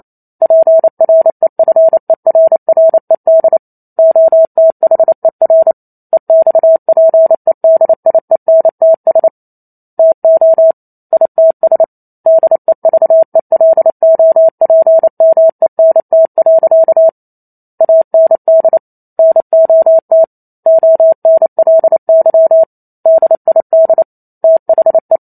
Morse code, heard as a keyed tone, about 28 wpm, 650 Hz. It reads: PREFERRED OTHER EXPEDIENTS TO ITS DEVELOPMENT. AND NOT ONLY DID THE
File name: Chapter 2 - 28 WPM